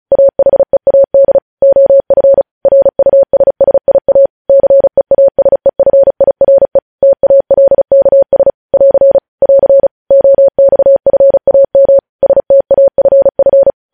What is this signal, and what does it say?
Morse code, heard as a keyed tone, about 36 wpm, 550 Hz. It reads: AHEAD OF RUSSIA CEASEFIRE TALKS <AR> <AR> OXFAM STAFF
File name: podcast / News headlines-1x-35x35